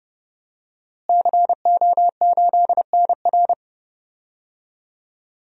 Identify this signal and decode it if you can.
Morse code, heard as a keyed tone, about 30 wpm, 700 Hz. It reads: CO8NR
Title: DXCC-100 Call Signs - 5 Characters - Repeated 3x - 30wpm